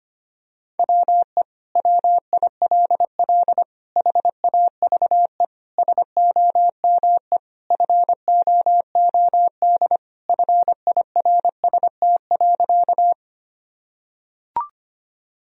Morse code, heard as a keyed tone, about 25 wpm, 700 Hz. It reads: WE WILL HAVE SOME FOOD FIRST.
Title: Sentences from Top 1000 Words - Review Part 1 - 25wpm